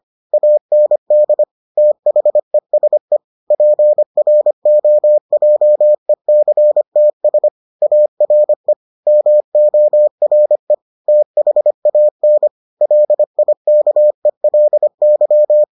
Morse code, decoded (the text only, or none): AND THESE PROJECTS ARE MORE THAN LIKELY